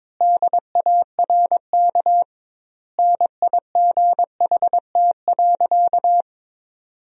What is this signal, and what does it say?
Morse code, heard as a keyed tone, about 22 wpm, 700 Hz. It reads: DARK NIGHT.
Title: 2x Word-Spacing - Sentences from Top 500 Words - 22wpm